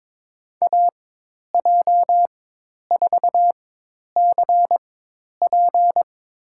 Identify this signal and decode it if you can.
Morse code, heard as a keyed tone, about 22 wpm, spaced as for 10 wpm, 700 Hz. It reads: AJ4CP
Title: USA Call Signs - 5 Characters - 4x Character Spacing - 22wpm